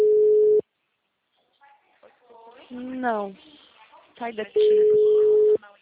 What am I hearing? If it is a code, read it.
Ringback tone (Europe, 425 Hz), with speech.